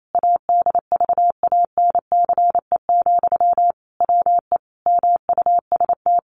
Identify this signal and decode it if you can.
Morse code, heard as a keyed tone, about 28 wpm, 700 Hz. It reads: ADVANCE, WE MUST
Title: Sentences from Top 1400 Words 28wpm